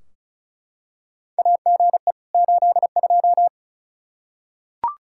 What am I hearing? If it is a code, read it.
Morse code, heard as a keyed tone, about 36 wpm, 700 Hz. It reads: AGE 82
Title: K1USN SST QSO Practice 35wpm